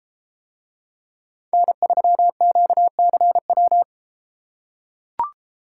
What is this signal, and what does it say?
Morse code, heard as a keyed tone, about 32 wpm, 700 Hz. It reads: N3QCW